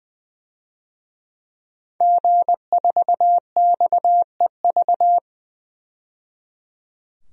Morse code, heard as a keyed tone, about 20 wpm, 700 Hz. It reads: G4XEV
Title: DXCC-100 Call Signs - 5 Characters - 20wpm